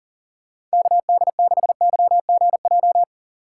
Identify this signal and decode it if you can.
Morse code, heard as a keyed tone, about 40 wpm, 700 Hz. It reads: KD6YGJ
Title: USA Call Signs - 6 Characters - 6x Character Spacing - 40wpm